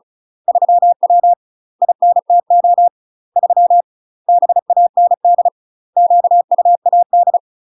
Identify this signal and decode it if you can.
Morse code, heard as a keyed tone, about 36 wpm, 700 Hz. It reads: 3W INTO 3 BAND QUAD
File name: UK MASTER QSOs TRAINING / 35WPM / UK MASTER QSOs 0004